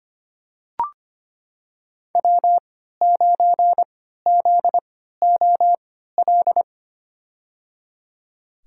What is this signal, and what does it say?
Morse code, heard as a keyed tone, about 25 wpm, spaced as for 14 wpm, 700 Hz. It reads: W9ZOL
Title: USA Call Signs - 5 Characters - 3x Character Spacing - 25wpm